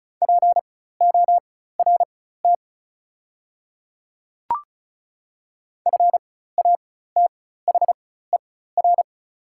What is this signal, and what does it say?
Morse code, heard as a keyed tone, about 34 wpm, spaced as for 16 wpm, 700 Hz. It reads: PORT FATHER
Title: 4x Character Spacing - Top 200 Words - 35wpm